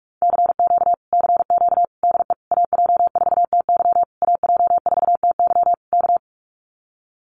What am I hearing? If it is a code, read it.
Morse code, heard as a keyed tone, about 46 wpm, 700 Hz. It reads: CQ CQ DE AJ4TY AJ4TY K
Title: Common QSO Phrases Encore 45wpm